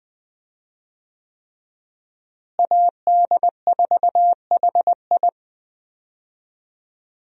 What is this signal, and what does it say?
Morse code, heard as a keyed tone, about 20 wpm, 700 Hz. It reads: AD4HI
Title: USA Call Signs - 5 Characters - Repeated 3x - 20wpm